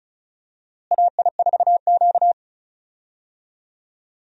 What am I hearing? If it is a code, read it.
Morse code, heard as a keyed tone, about 36 wpm, 700 Hz. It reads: AI4Q
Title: USA Call Signs - 4 Characters - Repeated 2x - 35wpm